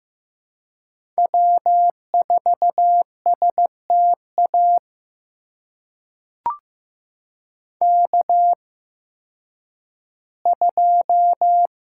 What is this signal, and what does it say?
Morse code, heard as a keyed tone, about 15 wpm, spaced as for 10 wpm, 700 Hz. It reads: W4STA K2
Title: USA Call Signs - 5 Characters - 8x Character Spacing - 15wpm